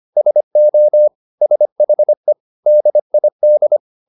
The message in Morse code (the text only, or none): SO SHE DID